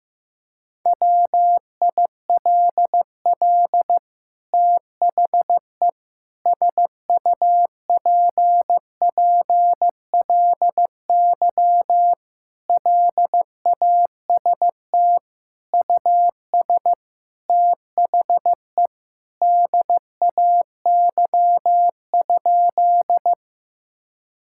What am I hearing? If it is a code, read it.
Morse code, heard as a keyed tone, about 15 wpm, 700 Hz. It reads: WILL THE SUPPLY LAST US THE DAY?
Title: Sentences from Top 1000 Words - Review Part 5 - 15wpm